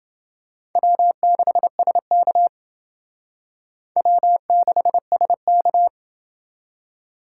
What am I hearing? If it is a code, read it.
Morse code, heard as a keyed tone, about 28 wpm, 700 Hz. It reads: W6SK W6SK